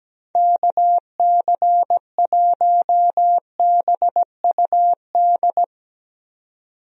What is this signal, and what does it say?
Morse code, heard as a keyed tone, about 17 wpm, 700 Hz. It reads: KC1BUD
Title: USA Call Signs - 6 Characters - 17wpm